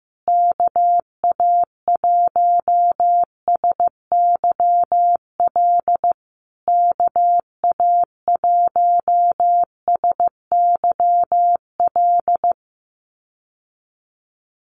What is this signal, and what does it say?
Morse code, heard as a keyed tone, about 15 wpm, 700 Hz. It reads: KA1SYL KA1SYL
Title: USA Call Signs Repeated 3x 15wpm